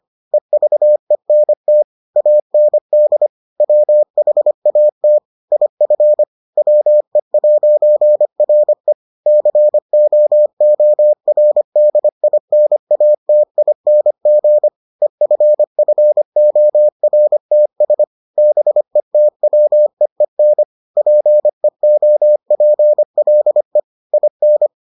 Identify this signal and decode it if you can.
Morse code, heard as a keyed tone, about 25 wpm, 600 Hz. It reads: EVENT AND WHAT IF WE'RE COORDINATING EFFORTS BETWEEN PEOPLE IN